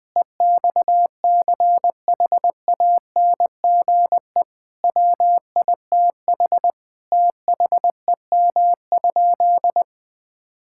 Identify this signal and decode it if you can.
Morse code, heard as a keyed tone, about 21 wpm, 700 Hz. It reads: EXCHANGE WITH THEM?
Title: Sentences from Top 1000 Words - Review Part 6 - 20wpm